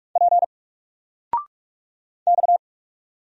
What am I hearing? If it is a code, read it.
Morse code, heard as a keyed tone, about 46 wpm, 700 Hz. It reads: P X
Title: Single Character (Letter:Number:Punctuation:Prosign) 45wpm